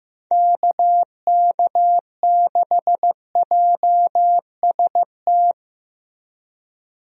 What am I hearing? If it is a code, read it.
Morse code, heard as a keyed tone, about 15 wpm, 700 Hz. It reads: KK6JST
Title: USA Call Signs - 6 Characters - Repeated 2x - 15wpm